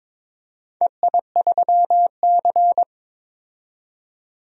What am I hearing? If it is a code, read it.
Morse code, heard as a keyed tone, about 22 wpm, 700 Hz. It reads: EI3C
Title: DXCC-100 Call Signs - 4 Characters - 22wpm